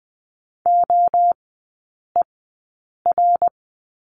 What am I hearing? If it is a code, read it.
Morse code, heard as a keyed tone, about 20 wpm, spaced as for 8 wpm, 700 Hz. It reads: OER
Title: 053 Lesson 10e - Letter r - Three Characters 20wpm